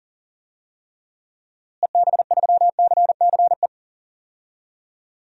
Morse code, heard as a keyed tone, about 40 wpm, 700 Hz. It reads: EB3CCE
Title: DXCC-100 Call Signs - 6 Characters - Repeated 3x - 40wpm